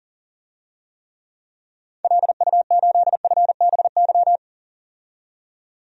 Morse code, heard as a keyed tone, about 37 wpm, 700 Hz. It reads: LU8FBY
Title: DXCC-100 Call Signs - 6 Characters - 40wpm